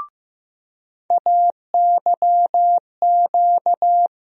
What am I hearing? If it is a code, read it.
Morse code, heard as a keyed tone, about 15 wpm, 700 Hz. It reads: AYQ
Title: USA Call Sign Postfixes - 3 Letters 15wpm